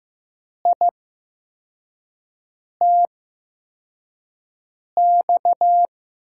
Morse code, heard as a keyed tone, about 15 wpm, spaced as for 4 wpm, 700 Hz. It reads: ITX
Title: USA Call Signs - 8x Character Spacing - 15wpm